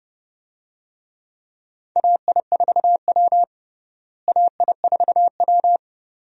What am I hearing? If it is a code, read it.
Morse code, heard as a keyed tone, about 29 wpm, 700 Hz. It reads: AI4W AI4W
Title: USA Call Signs - 4 Characters - Repeated 2x - 30wpm